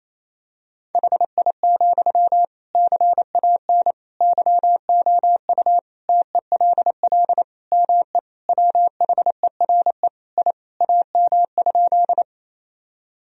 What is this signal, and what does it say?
Morse code, heard as a keyed tone, about 28 wpm, 700 Hz. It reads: HI, CAN YOU TELL ME WHERE I AM?
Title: Sentences from Top 1000 Words - Review Part 3 - 28wpm